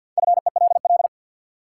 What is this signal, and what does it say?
Morse code, heard as a keyed tone, about 53 wpm, 700 Hz. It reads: FELL
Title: Sentences from Top 1600 Words - New Word Drill Alt - 50wpm